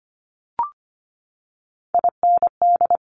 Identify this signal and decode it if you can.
Morse code, heard as a keyed tone, about 25 wpm, 700 Hz. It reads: IND
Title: Top Three Letter Combinations in Words 25wpm